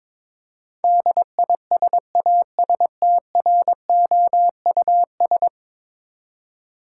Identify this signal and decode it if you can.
Morse code, heard as a keyed tone, about 22 wpm, 700 Hz. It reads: DISASTROUS
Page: morse-code-ninja-practice-22wpm